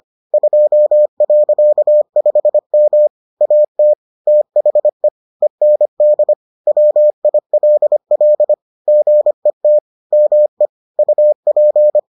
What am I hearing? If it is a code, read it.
Morse code, heard as a keyed tone, about 25 wpm, 600 Hz. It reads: 2.5M AT THE END WILL GET ME UP